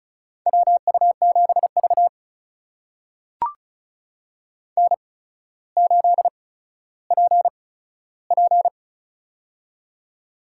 Morse code, heard as a keyed tone, about 36 wpm, spaced as for 15 wpm, 700 Hz. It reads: WU7V N8PP
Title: USA Call Signs - 4 Characters - 8x Character Spacing - 35wpm